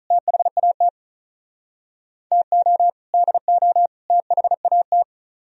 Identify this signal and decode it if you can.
Morse code, heard as a keyed tone, about 46 wpm, 700 Hz. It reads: THAT TO DO THAT